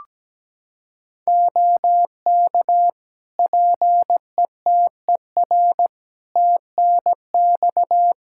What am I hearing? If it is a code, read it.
Morse code, heard as a keyed tone, about 17 wpm, 700 Hz. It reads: OK PETER TNX
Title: Practice QSO - Exchange 4 - Ending - 17wpm